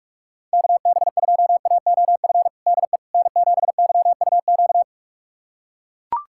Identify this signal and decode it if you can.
Morse code, heard as a keyed tone, about 43 wpm, 700 Hz. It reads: KB2AOF DE N7YUQ
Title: USA Callsign DE Callsign 45wpm